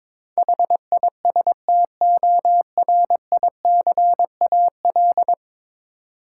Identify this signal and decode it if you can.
Morse code, heard as a keyed tone, about 21 wpm, 700 Hz. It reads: HISTORICAL
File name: practice / 22wpm / 10 Letter Words - First 3 Letters Given - 22wpm